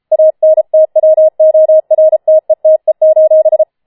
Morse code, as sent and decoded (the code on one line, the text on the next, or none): .- -. - .-- --- .-. - . - . ---...
ANTWORTETE: